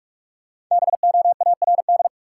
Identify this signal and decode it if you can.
Morse code, heard as a keyed tone, about 46 wpm, 700 Hz. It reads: BOARD
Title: Sentences from Top 700 Words - New Word Drill - 45wpm